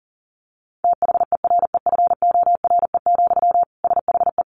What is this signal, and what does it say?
Morse code, heard as a keyed tone, about 40 wpm, 700 Hz. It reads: THEREFORE, SHE
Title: Sentences from Top 800 Words 40wpm